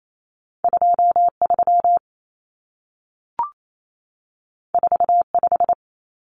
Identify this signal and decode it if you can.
Morse code, heard as a keyed tone, about 28 wpm, 700 Hz. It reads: 23 45
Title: Numbers - 2-Digit 28wpm